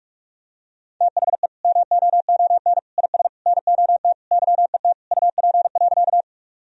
Morse code, heard as a keyed tone, about 46 wpm, 700 Hz. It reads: THE MOON IS NOT YET UP.
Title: Sentences from Top 1000 Words - Review Part 3 - 45wpm